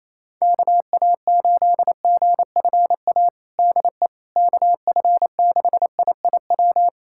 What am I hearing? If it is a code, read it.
Morse code, heard as a keyed tone, about 28 wpm, 700 Hz. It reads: KA8GFA DE KF6IIW
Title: USA Callsign DE Callsign 28wpm